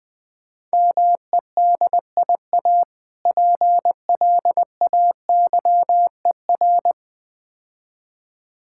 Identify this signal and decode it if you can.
Morse code, heard as a keyed tone, about 20 wpm, 690 Hz. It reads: MEDIA PLAYER
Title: Difficult Words - Speed-Racing with Hints - 20wpm